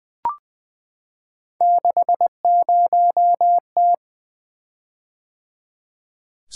Morse code, heard as a keyed tone, about 20 wpm, 700 Hz. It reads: 60T
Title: Three Letter-Number Combinations - 20wpm